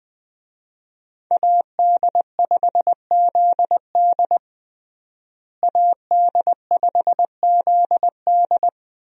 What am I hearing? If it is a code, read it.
Morse code, heard as a keyed tone, about 20 wpm, 700 Hz. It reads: AD5ZD AD5ZD
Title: USA Call Signs - 5 Characters - Repeated 3x - 20wpm